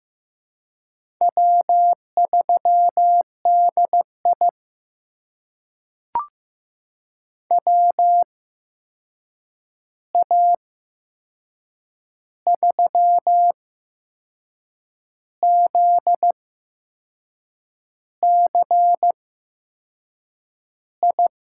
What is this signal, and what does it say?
Morse code, heard as a keyed tone, about 15 wpm, spaced as for 6 wpm, 700 Hz. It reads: W3DI WA3ZCI
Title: USA Call Signs - 8x Character Spacing - 15wpm